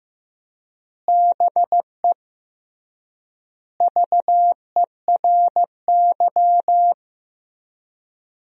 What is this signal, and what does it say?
Morse code, heard as a keyed tone, about 15 wpm, 700 Hz. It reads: BE VERY
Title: 3x Word-Spacing - Sentences from Top 200 Words - 15wpm